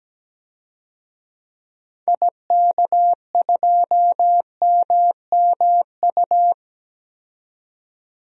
Morse code, heard as a keyed tone, about 17 wpm, 700 Hz. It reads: IK2MMU
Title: DXCC-100 Call Signs - 6 Characters - 17wpm